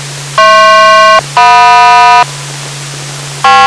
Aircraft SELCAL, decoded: HP-JM